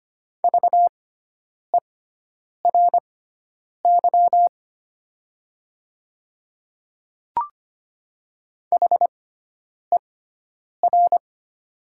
Morse code, heard as a keyed tone, about 25 wpm, spaced as for 9 wpm, 700 Hz. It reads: VERY HER
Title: 6x Character Spacing - Top 200 Words - 25wpm